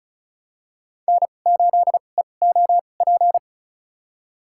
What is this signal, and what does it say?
Morse code, heard as a keyed tone, about 36 wpm, spaced as for 25 wpm, 700 Hz. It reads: N8EOP
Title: USA Call Signs - 2x Character Spacing - 35wpm